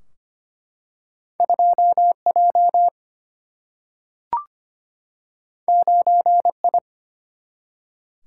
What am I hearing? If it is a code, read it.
Morse code, heard as a keyed tone, about 25 wpm, 700 Hz. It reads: 2 J 9 I
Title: Single-Letter Single-Number 25wpm